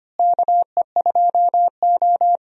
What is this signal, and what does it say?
Morse code, heard as a keyed tone, about 25 wpm, 700 Hz. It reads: KE2O
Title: USA Call Signs - 4 Characters - Repeated 2x - 25wpm